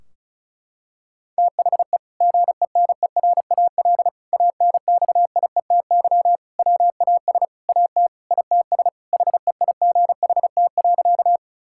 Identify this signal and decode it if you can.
Morse code, heard as a keyed tone, about 36 wpm, 700 Hz. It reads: THE GENERAL ANXIETY WAS AT ITS HEIGHT.
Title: Sentences from Top 1600 Words 35wpm